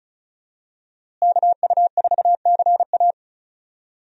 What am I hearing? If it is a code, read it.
Morse code, heard as a keyed tone, about 34 wpm, 700 Hz. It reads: KU4CA
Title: USA Call Signs - 5 Characters - Repeated 3x - 35wpm